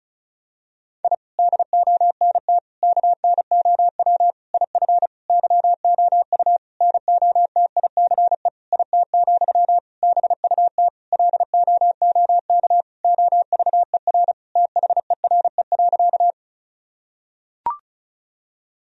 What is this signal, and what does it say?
Morse code, heard as a keyed tone, about 36 wpm, 700 Hz. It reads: I DONT KNOW IF YOU NOTICE IT, BUT LOOK OVER THERE.
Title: Sentences from Top 1000 Words - Review Part 2 - 35wpm